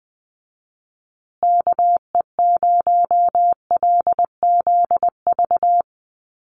Morse code, heard as a keyed tone, about 20 wpm, 700 Hz. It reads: KE0LZV
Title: USA Call Signs Repeated 2x 20wpm